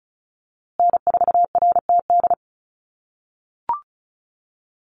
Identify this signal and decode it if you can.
Morse code, heard as a keyed tone, about 34 wpm, 700 Hz. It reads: N4RTD
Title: USA Call Signs Repeated 3x 35wpm